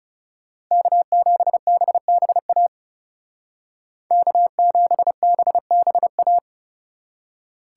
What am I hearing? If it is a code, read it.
Morse code, heard as a keyed tone, about 35 wpm, 700 Hz. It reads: K7BBA K7BBA
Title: Speed Racing - USA 1x3 Call Signs 30wpm